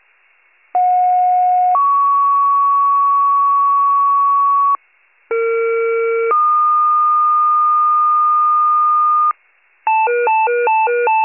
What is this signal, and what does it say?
Two-tone page, heard as a one-second tone, then a three-second tone, about 730 then 1120 Hz, about 460 then 1220 Hz. It is heard from 0.7 to 9.3 s.